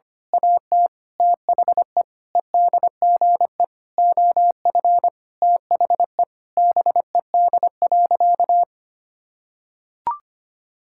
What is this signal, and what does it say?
Morse code, heard as a keyed tone, about 25 wpm, 700 Hz. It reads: AT THE EDGE OF THE BED.
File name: practice / 25wpm / Sentences with 4-Word Phrases - 25wpm